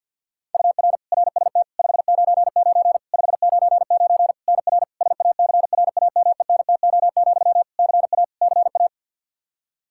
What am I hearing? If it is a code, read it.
Morse code, heard as a keyed tone, about 52 wpm, 700 Hz. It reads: UR RST 599 599 NR SACRAMENTO, CA CA
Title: K1USN SST QSO Practice 50wpm